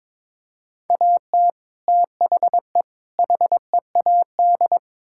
Morse code, heard as a keyed tone, about 22 wpm, 700 Hz. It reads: AT THE HEAD
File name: practice / 22wpm / Build A Phrase - 5-Word Phrases - 22wpm